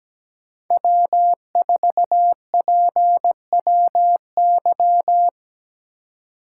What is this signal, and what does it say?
Morse code, heard as a keyed tone, about 17 wpm, 700 Hz. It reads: W4PWY
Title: USA Call Signs - 5 Characters - 2x Character Spacing - 17wpm